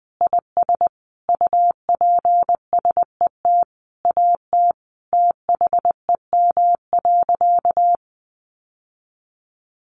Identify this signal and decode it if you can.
Morse code, heard as a keyed tone, about 20 wpm, 700 Hz. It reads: IS UPSET AT THEM.
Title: Sentences from Top 800 Words 20wpm